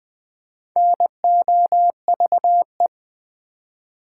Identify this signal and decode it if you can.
Morse code, heard as a keyed tone, about 20 wpm, 700 Hz. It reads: NOVE
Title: Build A Word - 2 Letters At A Time - 8-Letter Words - 20wpm